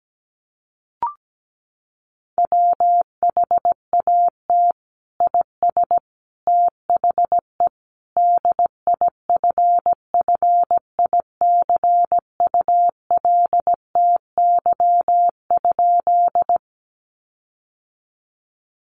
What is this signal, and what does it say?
Morse code, heard as a keyed tone, about 17 wpm, 700 Hz. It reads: WHAT IS THE DIFFICULTY?
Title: Sentences from Top 1500 Words 17wpm